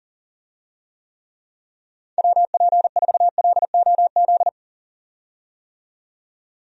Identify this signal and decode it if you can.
Morse code, heard as a keyed tone, about 40 wpm, 700 Hz. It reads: WP4LOZ
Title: USA Call Signs - 6 Characters - 40wpm